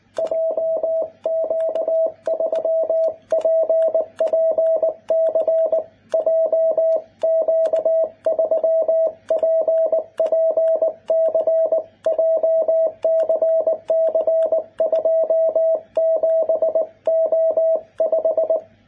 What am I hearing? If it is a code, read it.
Morse code, heard as a keyed tone, about 20 wpm, 650 Hz, with radio noise. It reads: JQ3PPC JQ3PPC JCC27O5
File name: jh8cbh / 2021cw